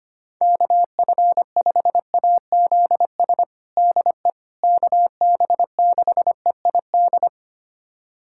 Morse code, heard as a keyed tone, about 25 wpm, 700 Hz. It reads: KF5AZS DE KB6EID
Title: USA Callsign DE Callsign - 6 Characters - 25wpm